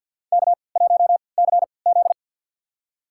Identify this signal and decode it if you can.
Morse code, heard as a keyed tone, about 46 wpm, spaced as for 27 wpm, 700 Hz. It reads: K1CZ
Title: USA Call Signs - 4 Characters - 3x Character Spacing - 50wpm